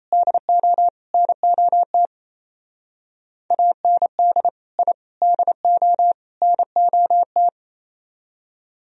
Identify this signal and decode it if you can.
Morse code, heard as a keyed tone, about 34 wpm, 700 Hz. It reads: DO NOT AND I DO NOT
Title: Speed-Racing - Sets of 4 Words 28wpm